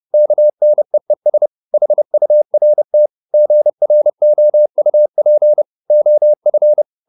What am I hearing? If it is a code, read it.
Morse code, heard as a keyed tone, about 30 wpm, 600 Hz. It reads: KNEES HURT GROUP OF